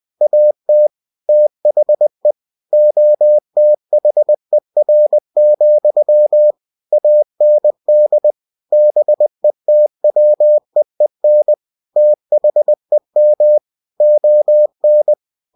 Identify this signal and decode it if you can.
Morse code, heard as a keyed tone, about 20 wpm, 600 Hz. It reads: AT THE OTHER, AND BETWEEN THEM ON
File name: cw-books / winnie-the-pooh / winnie-20wpm-page-0106